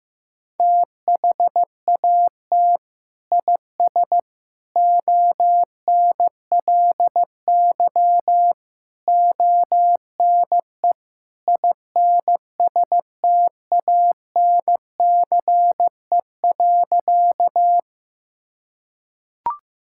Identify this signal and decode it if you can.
Morse code, heard as a keyed tone, about 15 wpm, 700 Hz. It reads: THAT IS ONLY ONE INSTANCE.
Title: Sentences from Top 1000 Words - Review Part 5 - 15wpm